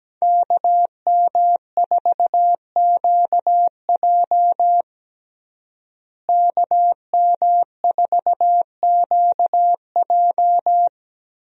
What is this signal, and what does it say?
Morse code, heard as a keyed tone, about 17 wpm, 700 Hz. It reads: KM4QJ KM4QJ
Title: USA Call Signs - 5 Characters - Repeated 2x - 17wpm